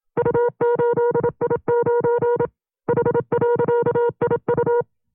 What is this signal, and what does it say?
Morse code, heard as a keyed tone, about 27 wpm, 480 Hz. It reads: U8I9 H.IU
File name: CW-practice/25WPM